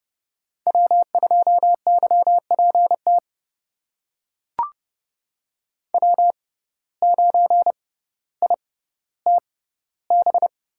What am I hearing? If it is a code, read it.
Morse code, heard as a keyed tone, about 30 wpm, spaced as for 15 wpm, 700 Hz. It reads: W2YPT W9ITB
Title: USA Call Signs - 6x Character Spacing - 30wpm